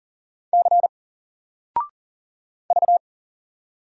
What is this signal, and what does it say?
Morse code, heard as a keyed tone, about 39 wpm, 700 Hz. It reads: C V
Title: Single Character (Letter:Number:Punctuation:Prosign) 40wpm